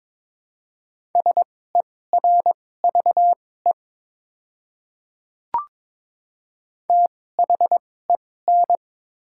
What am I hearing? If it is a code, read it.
Morse code, heard as a keyed tone, about 22 wpm, spaced as for 16 wpm, 700 Hz. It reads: SERVE THEN
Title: 2x Character Spacing - Top 500 Words - 22wpm